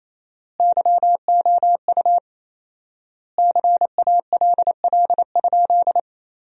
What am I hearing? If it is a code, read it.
Morse code, heard as a keyed tone, about 28 wpm, 700 Hz. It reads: YOU CALL?
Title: 4x Word-Spacing - Sentences from Top 100 Words - 28wpm